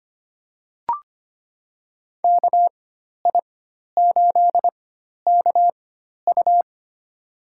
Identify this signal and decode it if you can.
Morse code, heard as a keyed tone, about 25 wpm, spaced as for 12 wpm, 700 Hz. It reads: KI8KU
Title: USA Call Signs - 5 Characters - 4x Character Spacing - 25wpm